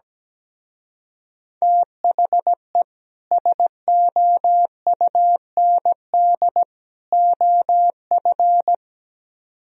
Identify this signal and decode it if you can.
Morse code, heard as a keyed tone, about 17 wpm, 700 Hz. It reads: THE SOUND OF
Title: Speed-Racing - Sets of 3 Words 17wpm